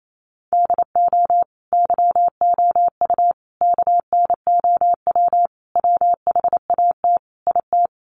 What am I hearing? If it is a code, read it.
Morse code, heard as a keyed tone, about 28 wpm, 700 Hz. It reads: DO YOU KNOW WHAT IT
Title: Sentences from Top 100 Words 28wpm